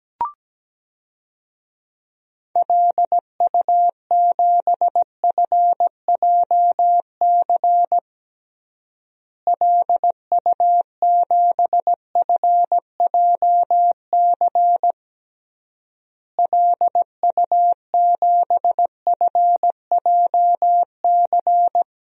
Morse code, heard as a keyed tone, about 17 wpm, 700 Hz. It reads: LU7FJC LU7FJC LU7FJC
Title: DXCC-100 Call Signs - 6 Characters - Repeated 3x - 17wpm